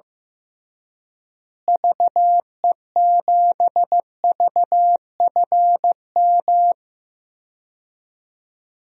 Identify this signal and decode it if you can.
Morse code, heard as a keyed tone, about 15 wpm, 700 Hz. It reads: VE7VFM
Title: DXCC-100 Call Signs - 6 Characters - Repeated 3x - 15wpm